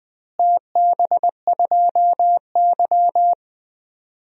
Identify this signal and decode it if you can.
Morse code, heard as a keyed tone, about 20 wpm, 700 Hz. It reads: TB2Y
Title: DXCC-100 Call Signs - 4 Characters - Repeated 2x - 20wpm